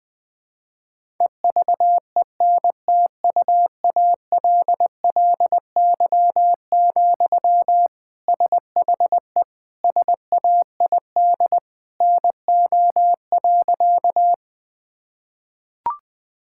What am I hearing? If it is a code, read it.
Morse code, heard as a keyed tone, about 21 wpm, 700 Hz. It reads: EVENTUALLY, SHE SAID NO.
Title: Sentences from Top 1000 Words - Review Part 6 - 20wpm